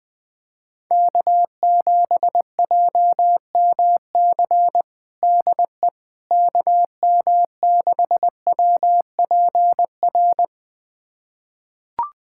Morse code, heard as a keyed tone, about 21 wpm, 700 Hz. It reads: K7JMC DE KM6WPR
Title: USA Callsign DE Callsign 20wpm